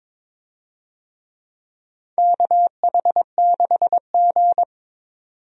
Morse code, heard as a keyed tone, about 21 wpm, 700 Hz. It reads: KH6G
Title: DXCC-100 Call Signs - 4 Characters - 22wpm